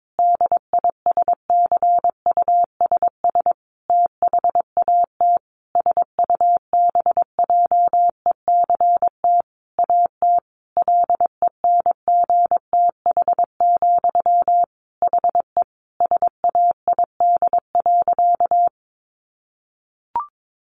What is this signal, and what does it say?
Morse code, heard as a keyed tone, about 22 wpm, 700 Hz. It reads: DISCUSS THAT SUBJECT AT LENGTH, HE SAID.
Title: Sentences from Top 1000 Words 22wpm